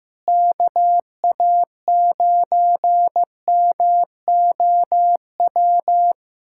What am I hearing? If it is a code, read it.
Morse code, heard as a keyed tone, about 15 wpm, 700 Hz. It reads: KA9MOW
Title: USA Callsign DE Callsign - 6 Characters - 15wpm